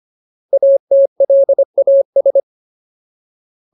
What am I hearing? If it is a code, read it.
Morse code, heard as a keyed tone, about 25 wpm, 550 Hz. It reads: ATLAS
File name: Ultimate Word List / 25WPM Ultimate